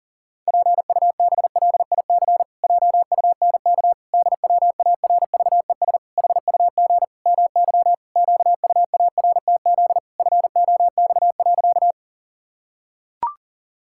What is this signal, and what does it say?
Morse code, heard as a keyed tone, about 37 wpm, 700 Hz. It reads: PUBLIC JUNK DWARVES HUG MY QUARTZ FOX.
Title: Pangram Challenge v2 - 40wpm